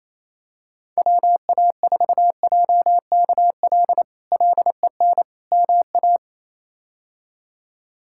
Morse code, heard as a keyed tone, about 27 wpm, 700 Hz. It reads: WA4JKL LEN MA
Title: K1USN SST QSO Practice - 28wpm